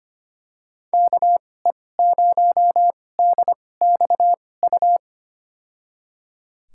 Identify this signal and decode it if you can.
Morse code, heard as a keyed tone, about 25 wpm, spaced as for 18 wpm, 700 Hz. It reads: KE0DXU
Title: USA Call Signs - 2x Character Spacing - 25wpm